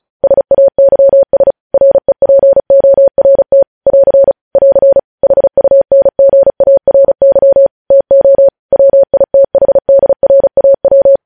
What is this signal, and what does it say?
Morse code, heard as a keyed tone, about 36 wpm, 550 Hz. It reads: SAYS REPORT <AR> <AR> HUNGARY TO WITHDRAW
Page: news headlines in morse code at 35 wpm